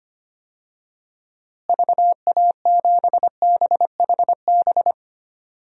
Morse code, heard as a keyed tone, about 25 wpm, 700 Hz. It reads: VA7BHB
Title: DXCC-100 Call Signs - 6 Characters - Repeated 3x - 25wpm